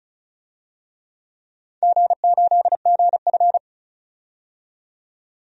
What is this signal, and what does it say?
Morse code, heard as a keyed tone, about 36 wpm, 700 Hz. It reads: G8GF
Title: DXCC-100 Call Signs - 4 Characters - 35wpm